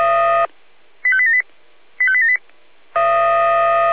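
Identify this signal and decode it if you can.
Five-tone selective call (ZVEI), heard as quick runs of five tones, tones of about 70 ms. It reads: 85878 85878
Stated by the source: ZVEI five-tone selective call received by SDR.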